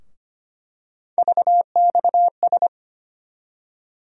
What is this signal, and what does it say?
Morse code, heard as a keyed tone, about 25 wpm, 700 Hz. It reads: VXS
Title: USA Call Sign Postfixes - 3 Letters 25wpm